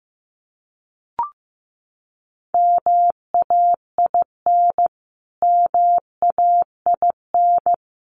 Morse code, heard as a keyed tone, about 15 wpm, 700 Hz. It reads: MAIN MAIN
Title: Top 500 Words Repeated 15wpm